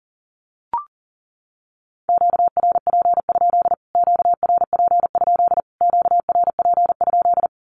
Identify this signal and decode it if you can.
Morse code, heard as a keyed tone, about 37 wpm, 700 Hz. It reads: QRP? QRP? QRP?
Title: Q-Codes Repeated 3x 40wpm